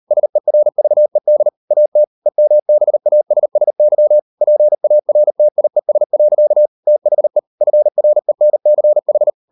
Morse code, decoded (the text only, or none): SERVED AT EMBASSY PARTIES. THE FRENCH